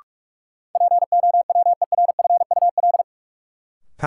Morse code, heard as a keyed tone, about 46 wpm, 700 Hz. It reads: POWERFUL
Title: Sentences from Top 1300 Words - New Word Drill Alt - 45wpm